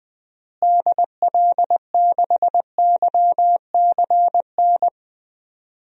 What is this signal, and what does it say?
Morse code, heard as a keyed tone, about 20 wpm, 700 Hz. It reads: DL6YCN
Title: DXCC-100 Call Signs - 6 Characters - Repeated 3x - 20wpm